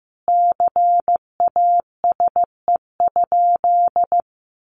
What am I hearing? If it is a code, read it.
Morse code, heard as a keyed tone, about 15 wpm, 700 Hz. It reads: CASE?
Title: Sentences from Top 1000 Words 15wpm